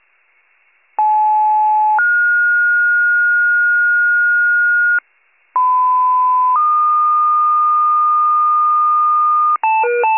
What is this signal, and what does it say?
Two-tone page, heard as a one-second tone, then a three-second tone, about 860 then 1430 Hz, about 1010 then 1220 Hz, with noise. It is heard from 1.0 to 9.6 s.
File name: pager / 94 RESCUE OFC / wcalert